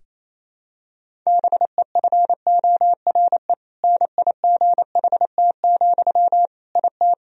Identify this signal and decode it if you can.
Morse code, heard as a keyed tone, about 28 wpm, 700 Hz. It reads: BEFORE NIGHT, IT
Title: Sentences from Top 1000 Words - Review Part 1 - 28wpm